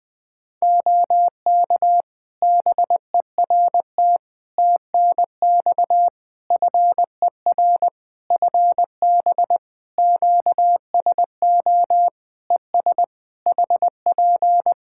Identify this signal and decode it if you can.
Morse code, heard as a keyed tone, about 20 wpm, 700 Hz. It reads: OK BERT TNX FER FB QSO ES HP
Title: Practice QSO - Exchange 4 - Ending - 20wpm